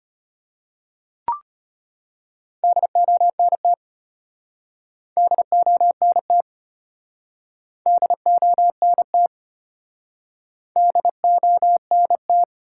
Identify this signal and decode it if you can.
Morse code, heard as a keyed tone, about 33 wpm, 700 Hz. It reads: DONT DONT DONT DONT